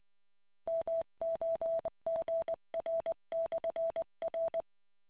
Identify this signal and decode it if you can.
Morse code, heard as a keyed tone, about 20 wpm, 660 Hz. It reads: MOGR/R